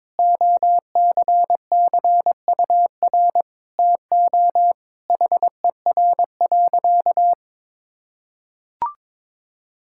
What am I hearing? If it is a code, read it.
Morse code, heard as a keyed tone, about 22 wpm, 700 Hz. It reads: OCCUR TO HER.
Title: Sentences from Top 1000 Words - Review Part 6 - 22wpm